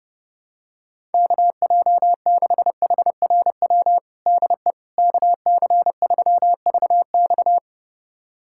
Morse code, heard as a keyed tone, about 29 wpm, 700 Hz. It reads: KJ6HRW DE KC3VX
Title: USA Callsign DE Callsign 30wpm